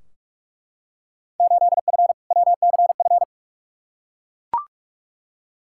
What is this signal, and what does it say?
Morse code, heard as a keyed tone, about 41 wpm, 700 Hz. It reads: 8F WCF
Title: Field Day Exchanges Encore - 45wpm